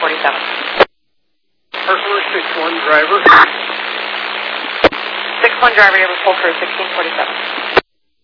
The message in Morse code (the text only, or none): none